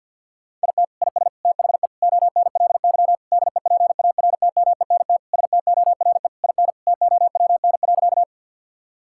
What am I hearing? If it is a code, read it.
Morse code, heard as a keyed tone, about 50 wpm, 700 Hz. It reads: IT IS THE ONLY DEPARTMENT STORE IN TOWN.
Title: Sentences from Top 1000 Words - Review Part 5 - 50wpm